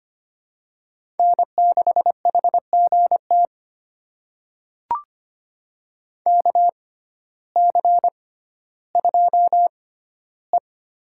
Morse code, heard as a keyed tone, about 25 wpm, spaced as for 14 wpm, 700 Hz. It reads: N6HGT KC2E
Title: USA Call Signs - 6x Character Spacing - 25wpm